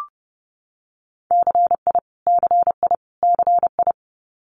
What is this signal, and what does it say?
Morse code, heard as a keyed tone, about 30 wpm, 700 Hz. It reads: CI CI CI
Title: Multi-letter Phonemes Repeated 3x 30wpm